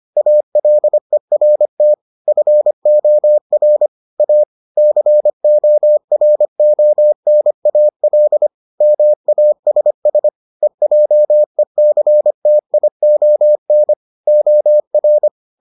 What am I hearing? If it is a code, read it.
Morse code, heard as a keyed tone, about 25 wpm, 600 Hz. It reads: ALERT FOR A CORONAL MASS EJECTION OR